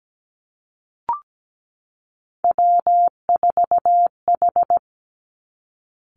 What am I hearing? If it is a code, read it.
Morse code, heard as a keyed tone, about 17 wpm, 700 Hz. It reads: W4H
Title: Three Challenging Letter-Number Combinations 17wpm